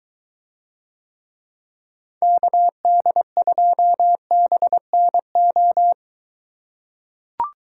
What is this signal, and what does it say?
Morse code, heard as a keyed tone, about 21 wpm, 700 Hz. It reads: KD2BNO